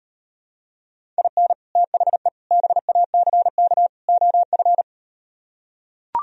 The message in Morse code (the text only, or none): IN THE BACK OF